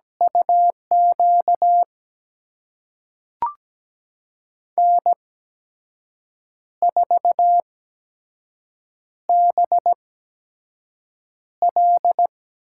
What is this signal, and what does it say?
Morse code, heard as a keyed tone, about 17 wpm, spaced as for 6 wpm, 700 Hz. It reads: UQ N4BL
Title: USA Call Signs - 8x Character Spacing - 17wpm